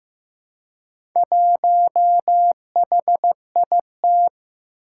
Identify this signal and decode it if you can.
Morse code, heard as a keyed tone, about 15 wpm, 700 Hz. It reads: 1HIT
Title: Four Challenging Letter-Number Combinations 15wpm